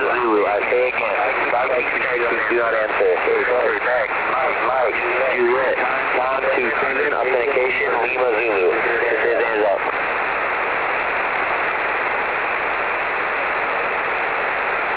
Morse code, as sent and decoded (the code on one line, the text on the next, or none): none